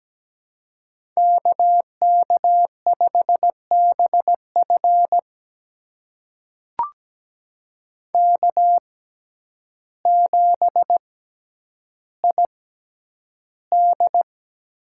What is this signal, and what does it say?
Morse code, heard as a keyed tone, about 17 wpm, spaced as for 9 wpm, 700 Hz. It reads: KK5BF K7ID
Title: USA Call Signs - 5 Characters - 6x Character Spacing - 17wpm